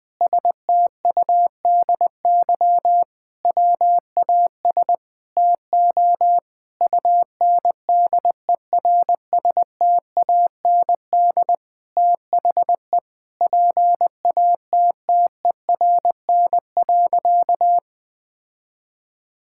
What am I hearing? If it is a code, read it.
Morse code, heard as a keyed tone, about 20 wpm, 700 Hz. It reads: STUDY WAS TO UNDERSTAND THE PATTERN.
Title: Sentences with 5-Word Phrases - 20wpm